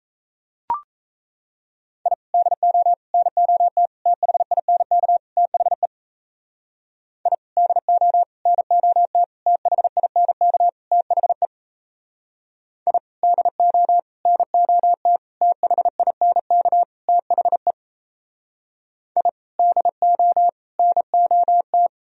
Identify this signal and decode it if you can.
Morse code, heard as a keyed tone, about 37 wpm, 700 Hz. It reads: I DO NOT THINK THE I DO NOT THINK THE I DO NOT THINK THE I DO NOT